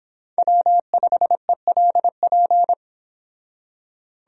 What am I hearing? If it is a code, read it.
Morse code, heard as a keyed tone, about 25 wpm, 700 Hz. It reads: W5ELP